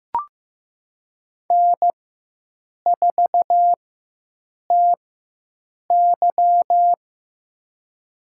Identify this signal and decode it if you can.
Morse code, heard as a keyed tone, about 15 wpm, spaced as for 7 wpm, 700 Hz. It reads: N4TY
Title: USA Call Signs - 4x Character Spacing - 15wpm